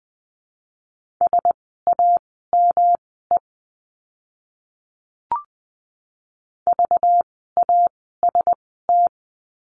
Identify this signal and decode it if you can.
Morse code, heard as a keyed tone, about 20 wpm, spaced as for 14 wpm, 700 Hz. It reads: SAME VAST
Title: 192 Lesson 31f - Letter v - Words 20wpm